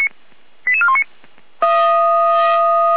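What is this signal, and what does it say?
Five-tone selective call (ZVEI), heard as quick runs of five tones, tones of about 70 ms. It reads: 88419
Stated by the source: ZVEI five-tone call received on an SDR, from feuerwehr-orlamuende (fire brigade Orlamünde).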